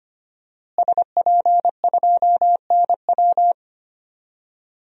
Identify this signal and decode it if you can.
Morse code, heard as a keyed tone, about 25 wpm, 700 Hz. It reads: SP2NW
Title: DXCC-100 Call Signs - 5 Characters - Repeated 2x - 25wpm